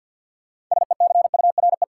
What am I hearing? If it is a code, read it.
Morse code, heard as a keyed tone, about 47 wpm, 700 Hz. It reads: SECURE
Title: Sentences from Top 1200 Words - New Word Drill Alt - 50wpm